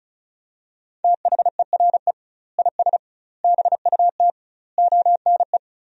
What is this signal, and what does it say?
Morse code, heard as a keyed tone, about 34 wpm, 700 Hz. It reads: THERE IS BUT ONE
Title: 2x Word-Spacing - Sentences from Top 100 Words - 35wpm